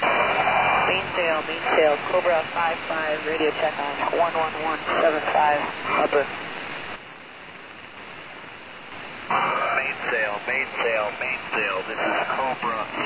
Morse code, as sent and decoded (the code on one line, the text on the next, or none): none